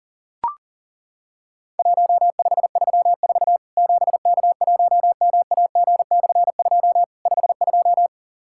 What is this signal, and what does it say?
Morse code, heard as a keyed tone, about 40 wpm, 700 Hz. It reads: 1534 7K1MAG/2 52
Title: Worked All Europe DX Contest - QTC Exchange - 40wpm